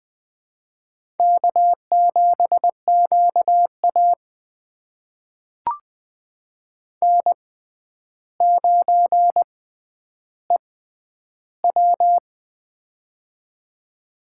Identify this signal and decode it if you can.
Morse code, heard as a keyed tone, about 20 wpm, spaced as for 10 wpm, 700 Hz. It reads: K7QA N9EW
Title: USA Call Signs - 4 Characters - 6x Character Spacing - 20wpm